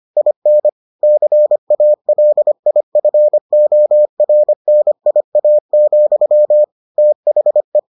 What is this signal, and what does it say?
Morse code, heard as a keyed tone, about 25 wpm, 600 Hz. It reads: IN CALIFORNIA, THE